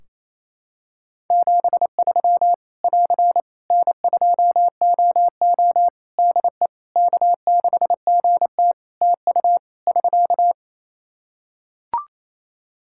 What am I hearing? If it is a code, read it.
Morse code, heard as a keyed tone, about 27 wpm, 700 Hz. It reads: 73 <AR> N2OO DE K6GT TU <SK>
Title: Practice QSO - Exchange 4 - Ending - 28wpm